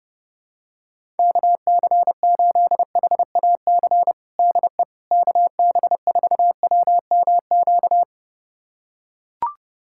Morse code, heard as a keyed tone, about 30 wpm, 700 Hz. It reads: KC8HAC DE KB4WMQ
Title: USA Callsign DE Callsign - 6 Characters - 30wpm